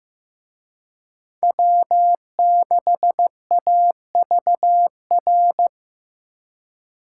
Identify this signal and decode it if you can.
Morse code, heard as a keyed tone, about 18 wpm, 700 Hz. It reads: W6AVR
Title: Speed Racing - USA 1x3 Call Signs 15wpm